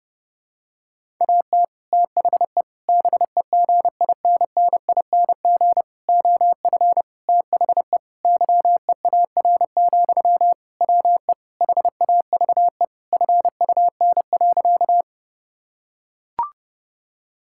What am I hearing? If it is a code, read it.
Morse code, heard as a keyed tone, about 31 wpm, 700 Hz. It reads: AT THE BEGINNING OF THE YEAR, WE HAVE FUN.
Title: Sentences from Top 1000 Words - Review Part 5 - 30wpm